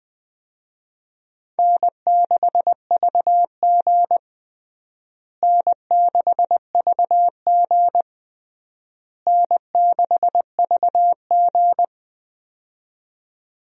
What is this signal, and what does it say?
Morse code, heard as a keyed tone, about 19 wpm, 700 Hz. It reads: N6VG N6VG N6VG
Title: USA Call Signs - 4 Characters - Repeated 3x - 20wpm